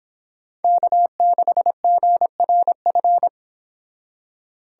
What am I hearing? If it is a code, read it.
Morse code, heard as a keyed tone, about 29 wpm, 700 Hz. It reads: K6GRF